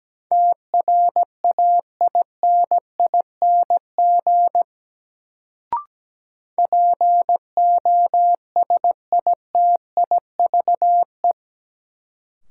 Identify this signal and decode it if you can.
Morse code, heard as a keyed tone, about 17 wpm, 700 Hz. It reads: TRAINING POSITIVE
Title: Sentences from Top 1400 Words - New Word Drill Alt - 17wpm